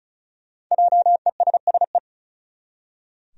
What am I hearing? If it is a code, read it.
Morse code, heard as a keyed tone, about 36 wpm, 700 Hz. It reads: JESSE
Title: USA Common Names - 35wpm